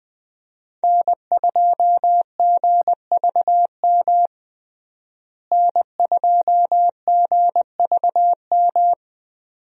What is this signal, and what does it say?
Morse code, heard as a keyed tone, about 20 wpm, 700 Hz. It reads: N2GVM N2GVM
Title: USA Call Signs - 5 Characters - Repeated 2x - 20wpm